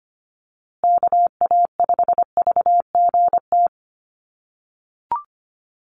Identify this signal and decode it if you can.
Morse code, heard as a keyed tone, about 25 wpm, 700 Hz. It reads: KA5VGT
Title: USA Call Signs Repeated 3x 25wpm